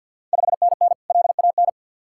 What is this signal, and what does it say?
Morse code, heard as a keyed tone, about 51 wpm, 690 Hz. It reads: 5NN LAN
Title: Ontario QSO Party Exchange 50wpm